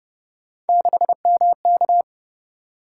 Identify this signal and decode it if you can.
Morse code, heard as a keyed tone, about 25 wpm, 700 Hz. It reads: 6MK